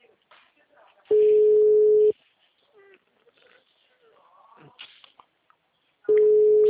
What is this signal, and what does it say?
Ringback tone (Europe, 425 Hz), with speech.